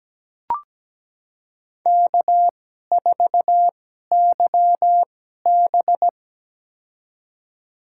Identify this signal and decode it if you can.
Morse code, heard as a keyed tone, about 17 wpm, spaced as for 12 wpm, 700 Hz. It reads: K4YB
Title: USA Call Signs - 4 Characters - 2x Character Spacing - 17wpm